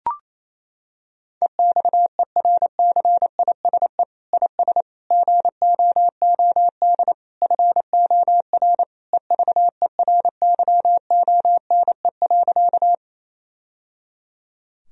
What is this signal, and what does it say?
Morse code, heard as a keyed tone, about 28 wpm, 700 Hz. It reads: EXERCISE IS GOOD FOR EVERYONE.
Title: Sentences from Top 1000 Words - Review Part 4 - 28wpm